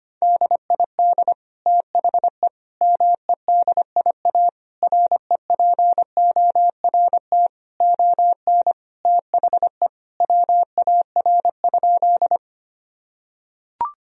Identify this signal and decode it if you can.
Morse code, heard as a keyed tone, about 25 wpm, 700 Hz. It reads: DID THE MEDIA REPORT ON THE WAR?
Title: Sentences from Top 1000 Words - Review Part 4 - 25wpm